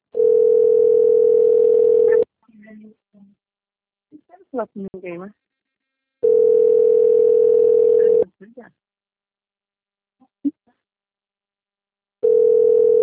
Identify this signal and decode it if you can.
Ringback tone (North America), 440 + 480 Hz, with speech.
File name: rg-1600-unknown- - - .12102